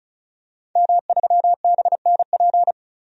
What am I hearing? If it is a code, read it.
Morse code, heard as a keyed tone, about 36 wpm, 700 Hz. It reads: M3BNP
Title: DXCC-100 Call Signs - 5 Characters - 35wpm